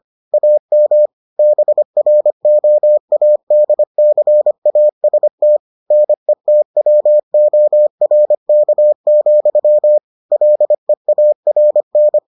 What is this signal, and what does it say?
Morse code, heard as a keyed tone, about 25 wpm, 600 Hz. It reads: AM BROADCAST NETWORK, LEARN